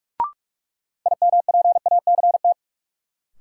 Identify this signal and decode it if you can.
Morse code, heard as a keyed tone, about 46 wpm, 700 Hz. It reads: IMPACT
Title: Sentences from Top 1000 Words - Review Part 6 - New Word Drill - 45wpm